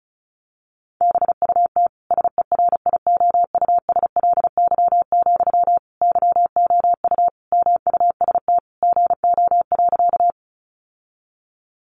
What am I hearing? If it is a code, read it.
Morse code, heard as a keyed tone, about 34 wpm, 700 Hz. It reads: BUT SERIOUSLY, YOU MUST GO.
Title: Sentences from Top 1200 Words 35wpm